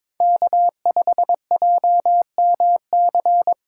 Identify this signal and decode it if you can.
Morse code, heard as a keyed tone, about 22 wpm, 700 Hz. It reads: K5JMC
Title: USA Call Signs - 5 Characters - 2x Character Spacing - 22wpm